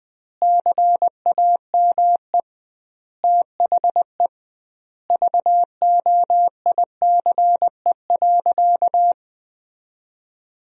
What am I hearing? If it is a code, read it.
Morse code, heard as a keyed tone, about 20 wpm, 700 Hz. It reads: CAME THE VOICE.
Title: 2x Word-Spacing - Sentences from Top 500 Words - 20wpm